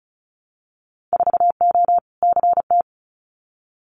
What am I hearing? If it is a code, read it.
Morse code, heard as a keyed tone, about 36 wpm, 700 Hz. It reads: 4O CT
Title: Winter Field Day Exchanges Encore 35wpm